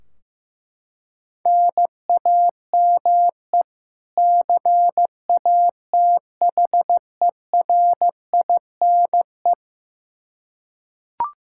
Morse code, heard as a keyed tone, about 16 wpm, 700 Hz. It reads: NAME CATHERINE
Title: Practice QSO - Exchange 1 - Beginning - 15wpm